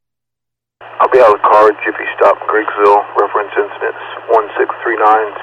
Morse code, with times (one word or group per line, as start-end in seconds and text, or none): none